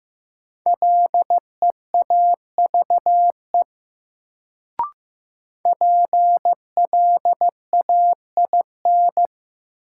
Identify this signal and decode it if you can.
Morse code, heard as a keyed tone, about 15 wpm, 700 Hz. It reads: LEAVE PLAIN
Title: Sentences from Top 400 Words - New Word Drill - 15wpm